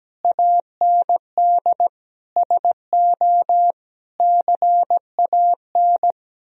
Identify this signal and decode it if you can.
Morse code, heard as a keyed tone, about 17 wpm, 700 Hz. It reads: AND SO CAN
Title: Sentences from Top 1000 Words - Review Part 2 - 17wpm